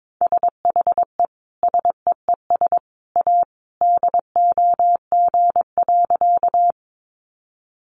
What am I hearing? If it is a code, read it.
Morse code, heard as a keyed tone, about 22 wpm, 700 Hz. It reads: SHE SEES A DOG.
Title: Easy Phrases 22wpm